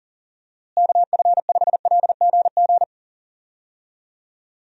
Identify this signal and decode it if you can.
Morse code, heard as a keyed tone, about 40 wpm, 700 Hz. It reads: KF5LGG
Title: USA Call Signs - 6 Characters - Repeated 2x - 40wpm